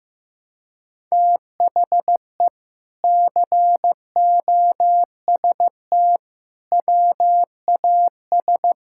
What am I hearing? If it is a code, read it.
Morse code, heard as a keyed tone, about 16 wpm, 700 Hz. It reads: THE COST WAS
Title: Sentences from Top 1000 Words - Review Part 6 - 15wpm